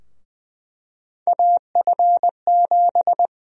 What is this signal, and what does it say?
Morse code, heard as a keyed tone, about 20 wpm, 700 Hz. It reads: AF7
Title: USA Call Sign Prefixes - 20wpm